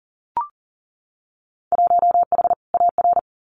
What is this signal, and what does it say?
Morse code, heard as a keyed tone, about 41 wpm, 700 Hz. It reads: 1H AR
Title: Winter Field Day Exchanges Encore 40wpm